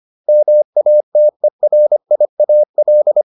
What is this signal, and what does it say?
Morse code, heard as a keyed tone, about 25 wpm, 600 Hz. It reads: MATERIAL